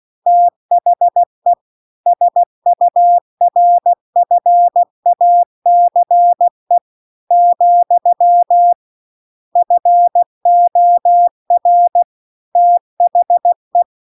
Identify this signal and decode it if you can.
Morse code, heard as a keyed tone, about 16 wpm, 700 Hz. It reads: THE SURFACE , FOR THE